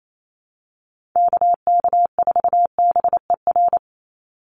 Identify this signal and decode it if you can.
Morse code, heard as a keyed tone, about 28 wpm, 700 Hz. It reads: KK4BER
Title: USA Call Signs 28wpm